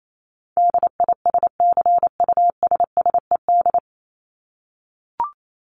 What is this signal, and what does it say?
Morse code, heard as a keyed tone, about 28 wpm, 700 Hz. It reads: DISCUSSED
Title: ED Suffix - Top 100 Words with Base Words 28wpm